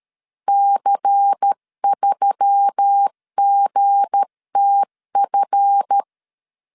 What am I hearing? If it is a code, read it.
Morse code, heard as a keyed tone, about 13 wpm, 800 Hz. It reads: C3GTF
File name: LEZIONI / 7 GF9Z / 7 GF 12wpm 2122